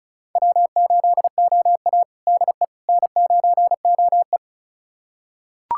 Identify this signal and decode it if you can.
Morse code, heard as a keyed tone, about 34 wpm, 700 Hz. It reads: W8OA DE N9OE
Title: USA Callsign DE Callsign - 4 Characters - 35wpm